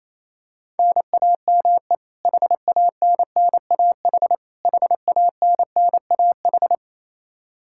Morse code, heard as a keyed tone, about 28 wpm, 700 Hz. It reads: NAME HANNAH HANNAH
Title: Practice QSO - Exchange 1 - Beginning - 28wpm